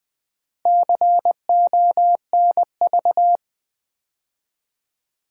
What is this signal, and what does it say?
Morse code, heard as a keyed tone, about 20 wpm, 700 Hz. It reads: CONV
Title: Build A Word - 2 Letters At A Time - 6-Letter Words - 20wpm